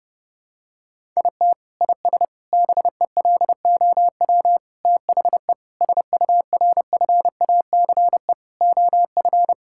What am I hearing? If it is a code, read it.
Morse code, heard as a keyed tone, about 30 wpm, 700 Hz. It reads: IT IS BELOW THE SURFACE OF
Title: Sentences from Top 1000 Words - Review Part 5 - 30wpm